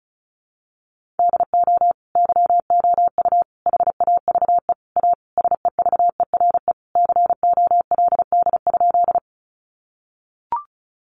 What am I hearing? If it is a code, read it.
Morse code, heard as a keyed tone, about 34 wpm, 700 Hz. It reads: DO YOU HAVE A SEVERE COLD?
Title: Sentences from Top 1400 Words 35wpm